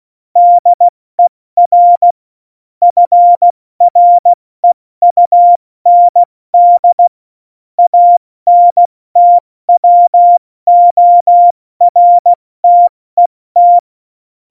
Morse code, heard as a keyed tone, about 16 wpm, 700 Hz. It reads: DER FREUND ANTWORTET
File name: Morsekurs / Witze / Witz 13. August 2019, 70 F80